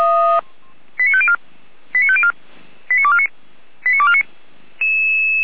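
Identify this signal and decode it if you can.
Five-tone selective call (ZVEI), heard as quick runs of five tones, tones of about 70 ms. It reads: 89504 89504 89249 89249
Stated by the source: ZVEI five-tone selective call 89250 received by SDR, archived by feuerwehr-orlamuende (fire brigade Orlamünde).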